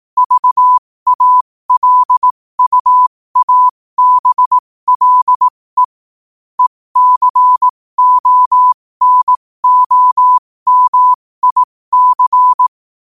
Morse code, heard as a keyed tone, about 18 wpm, spaced as for 15 wpm, 1000 Hz. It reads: VALUABLE ECONOMIC